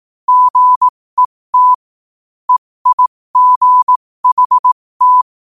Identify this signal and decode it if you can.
Morse code, heard as a keyed tone, about 18 wpm, spaced as for 15 wpm, 1000 Hz. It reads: GET EIGHT
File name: qotdfiles / 15wpm